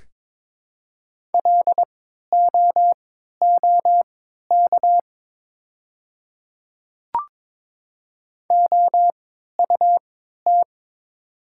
Morse code, heard as a keyed tone, about 22 wpm, spaced as for 13 wpm, 700 Hz. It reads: LOOK OUT
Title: 3x Character Spacing - Top 100 Words - 22wpm